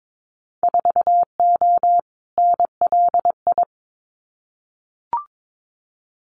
Morse code, heard as a keyed tone, about 22 wpm, 700 Hz. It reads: 4O NLI
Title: Winter Field Day Exchanges Encore 22wpm